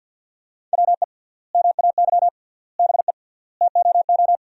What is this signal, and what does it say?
Morse code, heard as a keyed tone, about 50 wpm, 700 Hz. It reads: WE MAY BE TOO
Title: 3x Word-Spacing - Sentences from Top 300 Words - 50wpm